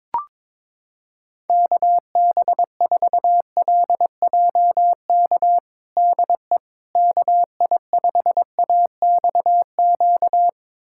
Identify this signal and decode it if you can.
Morse code, heard as a keyed tone, about 22 wpm, 700 Hz. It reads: KB4LJK DE KI5AXQ
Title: USA Callsign DE Callsign - 6 Characters - 22wpm